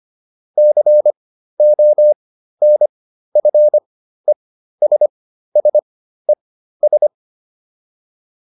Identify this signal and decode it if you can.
Morse code, heard as a keyed tone, about 25 wpm, spaced as for 13 wpm, 600 Hz. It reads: CONFESSES